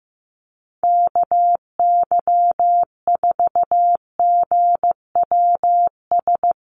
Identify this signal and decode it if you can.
Morse code, heard as a keyed tone, about 15 wpm, 700 Hz. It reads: KY4GWS
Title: USA Call Signs Repeated 3x 15wpm